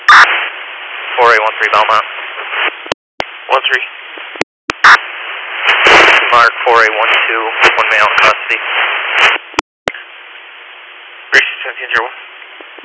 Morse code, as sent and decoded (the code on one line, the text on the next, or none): none